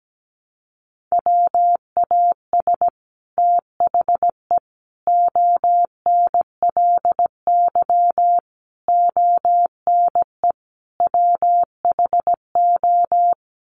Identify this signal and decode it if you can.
Morse code, heard as a keyed tone, about 17 wpm, 700 Hz. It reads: WAS THE ONLY ONE WHO
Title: Sets of 5 Words Encore2 17wpm